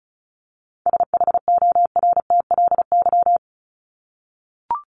Morse code, heard as a keyed tone, about 34 wpm, 700 Hz. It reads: SHORTLY
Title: LY Suffix - Top 100 Words with Base Words 35wpm